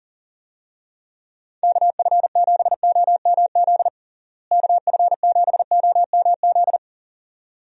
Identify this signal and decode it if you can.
Morse code, heard as a keyed tone, about 37 wpm, 700 Hz. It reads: KF7OMZ KF7OMZ
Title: USA Call Signs - 6 Characters - Repeated 2x - 40wpm